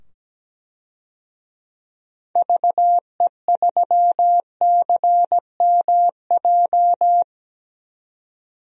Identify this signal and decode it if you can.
Morse code, heard as a keyed tone, about 17 wpm, 700 Hz. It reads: VE3CMJ
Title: DXCC-100 Call Signs - 6 Characters - 17wpm